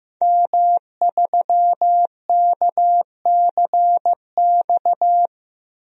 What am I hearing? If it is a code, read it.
Morse code, heard as a keyed tone, about 15 wpm, 700 Hz. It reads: M3KCX
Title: DXCC-100 Call Signs - 5 Characters - 15wpm